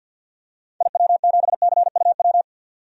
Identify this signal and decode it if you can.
Morse code, heard as a keyed tone, about 50 wpm, 700 Hz. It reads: IW7CUW
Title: DXCC-100 Call Signs - 6 Characters - 50wpm